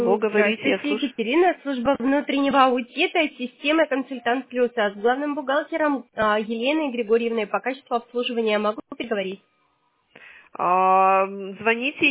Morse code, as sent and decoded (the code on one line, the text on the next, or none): none